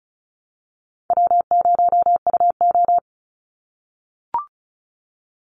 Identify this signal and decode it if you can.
Morse code, heard as a keyed tone, about 34 wpm, 700 Hz. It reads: W0UO
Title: Call Signs - 35wpm